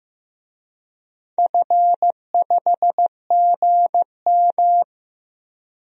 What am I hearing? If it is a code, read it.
Morse code, heard as a keyed tone, about 15 wpm, 700 Hz. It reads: F5GM
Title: DXCC-100 Call Signs - 4 Characters - Repeated 2x - 15wpm